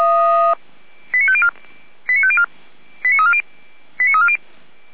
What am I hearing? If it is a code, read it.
Five-tone selective call (ZVEI), heard as quick runs of five tones, tones of about 70 ms. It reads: 89504 89504 89340 89340